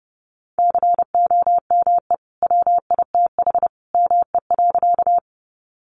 Morse code, heard as a keyed tone, about 30 wpm, 700 Hz. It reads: COME WITH ME.
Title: Sentences from Top 200 Words 30wpm